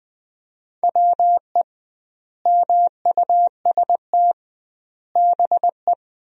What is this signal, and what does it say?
Morse code, heard as a keyed tone, about 21 wpm, 700 Hz. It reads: WE MUST BE
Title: 2x Word-Spacing - Sentences from Top 500 Words - 20wpm